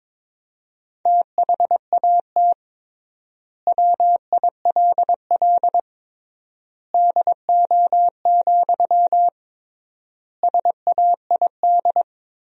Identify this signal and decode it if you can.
Morse code, heard as a keyed tone, about 22 wpm, 700 Hz. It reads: THAT WILL DO, SAID
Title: 3x Word-Spacing - Sentences from Top 100 Words - 22wpm